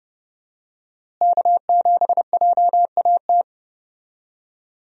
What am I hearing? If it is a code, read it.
Morse code, heard as a keyed tone, about 30 wpm, 700 Hz. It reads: K7JAT
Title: USA Call Signs - 5 Characters - Repeated 2x - 30wpm